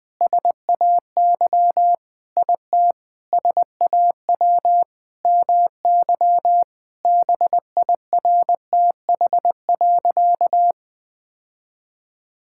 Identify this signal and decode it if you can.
Morse code, heard as a keyed tone, about 20 wpm, 700 Hz. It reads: SAY IT SAW MY BIRTH.
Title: Sentences from Top 1000 Words - Review Part 5 - 20wpm